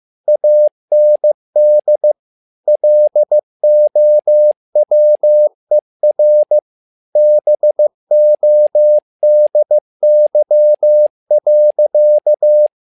Morse code, heard as a keyed tone, about 15 wpm, 600 Hz. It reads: AND LOWER BODY.
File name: learning center / tower trouble / tower trouble 15wpm